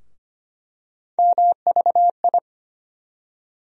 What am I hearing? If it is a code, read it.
Morse code, heard as a keyed tone, about 25 wpm, 700 Hz. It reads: MVI
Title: USA Call Sign Postfixes - 3 Letters 25wpm